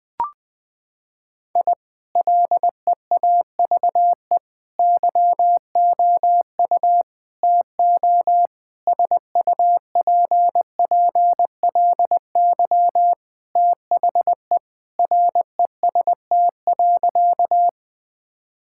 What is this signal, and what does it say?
Morse code, heard as a keyed tone, about 21 wpm, 700 Hz. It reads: I LEAVE YOU TO SUPPLY THE REST.
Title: Sentences from Top 1000 Words - Review Part 5 - 20wpm